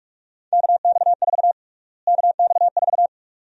Morse code, heard as a keyed tone, about 41 wpm, 700 Hz. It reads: KX4 KX4
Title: USA Call Sign Prefixes - Repeated 2x - 45wpm